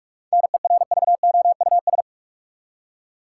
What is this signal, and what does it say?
Morse code, heard as a keyed tone, about 45 wpm, 700 Hz. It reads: NERVOUS
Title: Seven Letter Words - 45wpm